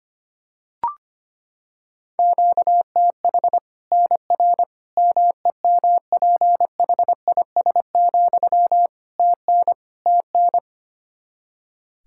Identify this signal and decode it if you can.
Morse code, heard as a keyed tone, about 25 wpm, 700 Hz. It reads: QTH NR MEMPHIS, TN TN
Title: Practice QSO - Exchange 1 - Beginning - 25wpm